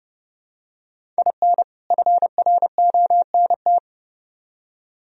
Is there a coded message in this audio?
Yes: Morse code from 1.2 to 3.8 s, heard as a keyed tone, about 29 wpm, 700 Hz. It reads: IN FRONT